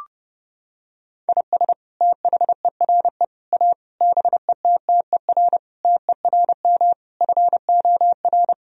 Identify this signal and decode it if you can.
Morse code, heard as a keyed tone, about 30 wpm, 700 Hz. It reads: IS THERE A BETTER TERM FOR
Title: Sentences from Top 1000 Words - Review Part 4 - 30wpm